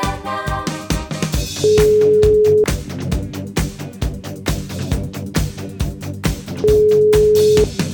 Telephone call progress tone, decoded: ringback tone

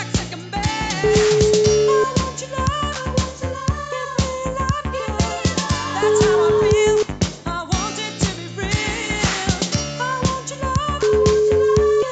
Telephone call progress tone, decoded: ringback tone

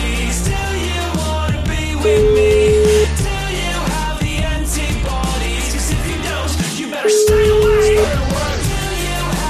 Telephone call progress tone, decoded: ringback tone

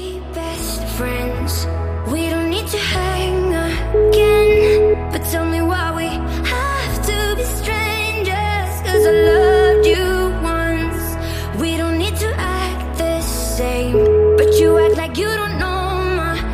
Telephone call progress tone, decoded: ringback tone

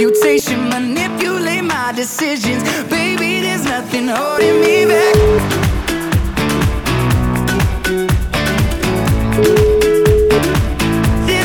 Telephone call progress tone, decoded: ringback tone

